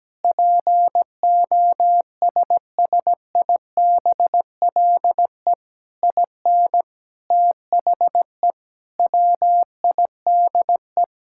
Morse code, heard as a keyed tone, about 17 wpm, 700 Hz. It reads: POSSIBLE IN THE WIDE
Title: Sentences from Top 1000 Words - Review Part 6 - 17wpm